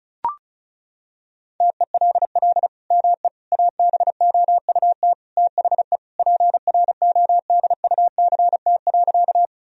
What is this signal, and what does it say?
Morse code, heard as a keyed tone, about 36 wpm, 700 Hz. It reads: TELL ME ABOUT THE PRODUCT.
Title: Sentences from Top 1000 Words - Review Part 2 - 35wpm